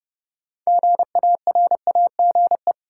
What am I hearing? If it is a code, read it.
Morse code, heard as a keyed tone, about 30 wpm, 700 Hz. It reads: GARAGE
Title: Sentences from Top 1600 Words - New Word Drill Alt - 30wpm